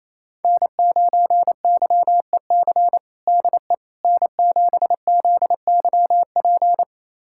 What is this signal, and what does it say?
Morse code, heard as a keyed tone, about 28 wpm, 700 Hz. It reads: N9YEC DE N7ZYP
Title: USA Callsign DE Callsign - 5 Characters - 28wpm